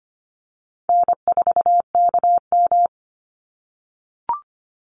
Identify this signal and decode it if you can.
Morse code, heard as a keyed tone, about 25 wpm, 700 Hz. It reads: N4KM
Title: Easy Call Signs - 25wpm